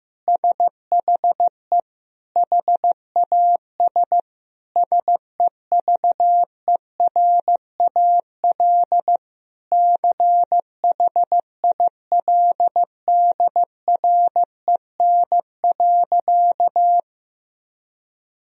Morse code, heard as a keyed tone, about 15 wpm, 700 Hz. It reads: SHE HAS SEVERAL CHILDREN.
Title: Sentences from Top 1000 Words - Review Part 2 - 15wpm